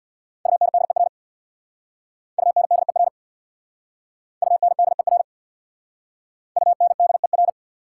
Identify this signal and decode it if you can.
Morse code, heard as a keyed tone, about 69 wpm, 690 Hz. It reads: UNDER UNDER UNDER UNDER